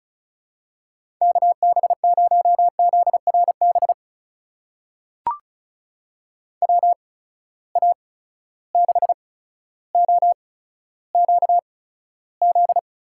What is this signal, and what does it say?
Morse code, heard as a keyed tone, about 36 wpm, spaced as for 15 wpm, 700 Hz. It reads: KB0ZRB WA6OQZ
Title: USA Call Signs - 6 Characters - 8x Character Spacing - 35wpm